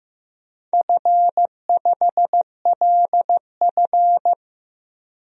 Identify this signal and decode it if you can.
Morse code, heard as a keyed tone, about 15 wpm, 700 Hz. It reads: F5LF
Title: DXCC-100 Call Signs - 4 Characters - Repeated 3x - 15wpm